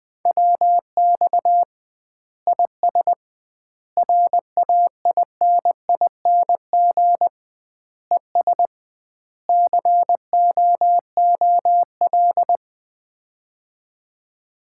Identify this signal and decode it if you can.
Morse code, heard as a keyed tone, about 20 wpm, 700 Hz. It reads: WX IS RAINING ES COOL
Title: Common QSO Phrases - 2x Word-Spacing - 20wpm